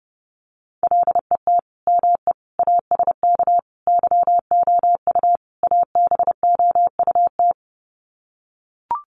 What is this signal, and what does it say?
Morse code, heard as a keyed tone, about 30 wpm, 700 Hz. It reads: LET ME ASK YOU ABOUT
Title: Sets of 5 Words 30wpm